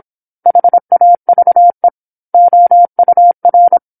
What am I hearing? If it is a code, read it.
Morse code, heard as a keyed tone, about 27 wpm, 700 Hz. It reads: HAVE OUR
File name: 08-Chapter-08 26 WPM